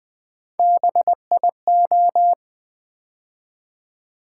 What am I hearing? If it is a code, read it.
Morse code, heard as a keyed tone, about 20 wpm, 700 Hz. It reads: BIO
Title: Build A Word - 3 Letters At A Time - 9-Letter Words - 20wpm